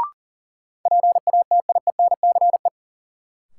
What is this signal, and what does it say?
Morse code, heard as a keyed tone, about 40 wpm, 700 Hz. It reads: PATIENCE
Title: Sentences from Top 1300 Words - New Word Drill Alt - 40wpm